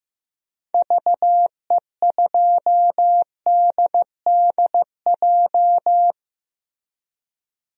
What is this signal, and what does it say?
Morse code, heard as a keyed tone, about 15 wpm, 700 Hz. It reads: VE2DDJ
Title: DXCC-100 Call Signs - 6 Characters - Repeated 3x - 15wpm